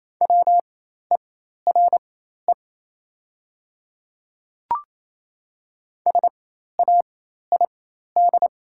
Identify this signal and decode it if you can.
Morse code, heard as a keyed tone, about 28 wpm, spaced as for 13 wpm, 700 Hz. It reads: WERE SAID
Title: 4x Character Spacing - Top 200 Words - 28wpm